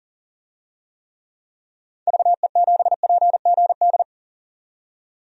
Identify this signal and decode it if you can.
Morse code, heard as a keyed tone, about 40 wpm, 700 Hz. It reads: VE7PGD
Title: DXCC-100 Call Signs - 6 Characters - 40wpm